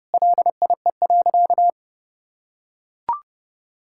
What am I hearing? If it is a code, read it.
Morse code, heard as a keyed tone, about 30 wpm, 700 Hz. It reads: LIE.
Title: Sentences from Top 1600 Words 30wpm